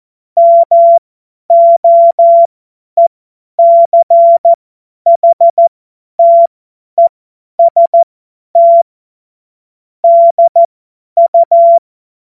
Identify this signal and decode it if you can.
Morse code, heard as a keyed tone, about 14 wpm, spaced as for 10 wpm, 680 Hz. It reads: MOECHTEST DU
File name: Morsekurs / Witze / Witz 18. Juli 2023 50 F70